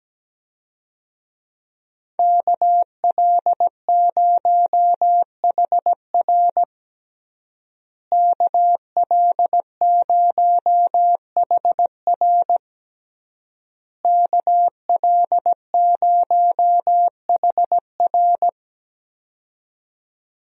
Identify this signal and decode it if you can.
Morse code, heard as a keyed tone, about 17 wpm, 700 Hz. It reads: KL0HR KL0HR KL0HR
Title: USA Call Signs - 5 Characters - Repeated 3x - 17wpm